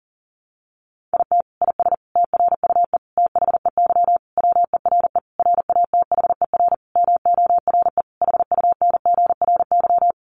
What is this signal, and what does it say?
Morse code, heard as a keyed tone, about 40 wpm, 700 Hz. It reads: IT IS TRUE THEY WERE RATHER MORE HUNGRY
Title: Sentences from Top 1400 Words 40wpm